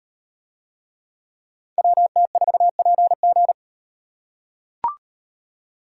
Morse code, heard as a keyed tone, about 36 wpm, 700 Hz. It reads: WT4PG